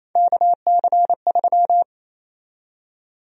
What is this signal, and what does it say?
Morse code, heard as a keyed tone, about 28 wpm, 700 Hz. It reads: KC3
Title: USA Call Sign Prefixes - Repeated 2x - 28wpm